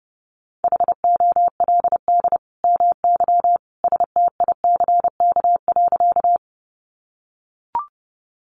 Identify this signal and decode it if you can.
Morse code, heard as a keyed tone, about 30 wpm, 700 Hz. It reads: HOLD MY STICK.
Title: Sentences from Top 1200 Words 30wpm